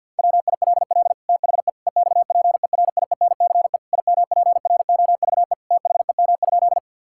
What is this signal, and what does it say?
Morse code, heard as a keyed tone, about 50 wpm, 700 Hz. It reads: WILL THE EXPERIENCE IMPROVE THEM?
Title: Sentences from Top 1000 Words - Review Part 5 - 50wpm